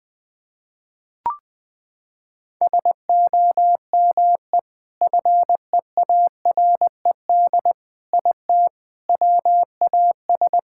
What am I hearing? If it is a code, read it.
Morse code, heard as a keyed tone, about 20 wpm, 700 Hz. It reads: SOME FEARED IT WAS
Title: Sentences with 5-Word Phrases - 20wpm